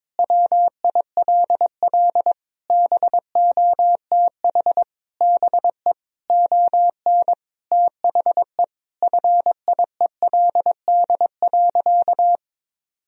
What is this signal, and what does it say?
Morse code, heard as a keyed tone, about 22 wpm, 700 Hz. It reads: WILL BOTH BE ON THE FIELD.
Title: Sentences from Top 1000 Words - Review Part 2 - 22wpm